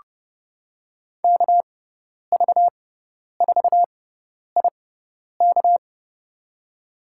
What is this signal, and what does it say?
Morse code, heard as a keyed tone, about 29 wpm, spaced as for 10 wpm, 700 Hz. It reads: KV4IK
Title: USA Call Signs - 6x Character Spacing - 30wpm